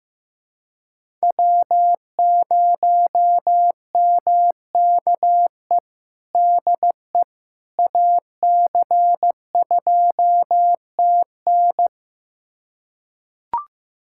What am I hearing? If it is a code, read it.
Morse code, heard as a keyed tone, about 15 wpm, 700 Hz. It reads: W0MKE DE AC2TN
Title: USA Callsign DE Callsign - 5 Characters - 15wpm